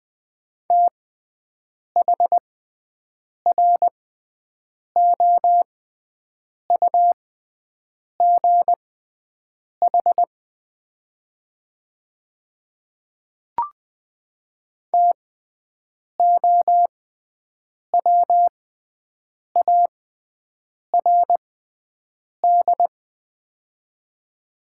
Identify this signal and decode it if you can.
Morse code, heard as a keyed tone, about 20 wpm, spaced as for 7 wpm, 700 Hz. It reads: THROUGH TOWARD
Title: 6x Character Spacing - Top 500 Words - 20wpm